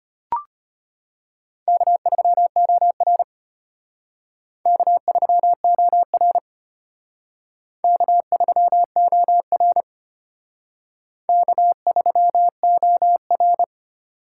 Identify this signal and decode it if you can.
Morse code, heard as a keyed tone, about 32 wpm, 700 Hz. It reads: K3OR K3OR K3OR K3OR